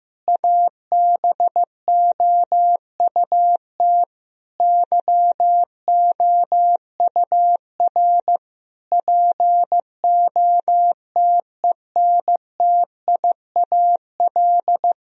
Morse code, heard as a keyed tone, about 15 wpm, 700 Hz. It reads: ABOUT YOUR POTENTIAL
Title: Sentences from Top 1000 Words - Review Part 4 - 15wpm